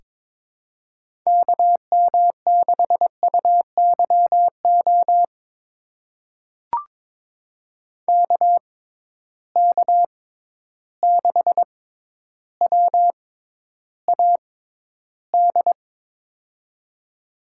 Morse code, heard as a keyed tone, about 22 wpm, spaced as for 11 wpm, 700 Hz. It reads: KM6UYO KK6WAD
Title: USA Call Signs - 6 Characters - 6x Character Spacing - 22wpm